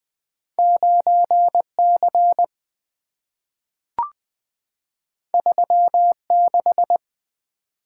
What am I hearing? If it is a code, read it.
Morse code, heard as a keyed tone, about 20 wpm, 700 Hz. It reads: 9 C 3 6
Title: Two Letter-Number Combinations - 20wpm